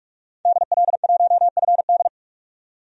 Morse code, heard as a keyed tone, about 44 wpm, 690 Hz. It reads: DL1FD
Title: DXCC-100 Call Signs - 5 Characters - 45wpm